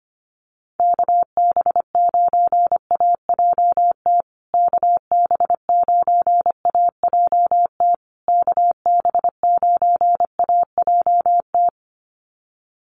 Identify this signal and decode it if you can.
Morse code, heard as a keyed tone, about 25 wpm, 700 Hz. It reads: KB9AJT KB9AJT KB9AJT
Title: USA Call Signs Repeated 3x 25wpm